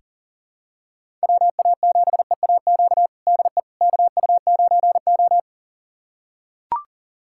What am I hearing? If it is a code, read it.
Morse code, heard as a keyed tone, about 37 wpm, 700 Hz. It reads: WA7EAQ DE KU9O
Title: USA Callsign DE Callsign 40wpm